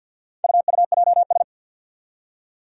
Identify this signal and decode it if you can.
Morse code, heard as a keyed tone, about 47 wpm, 700 Hz. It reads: UUJS
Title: Four Challenging Letter-Number Combinations 50wpm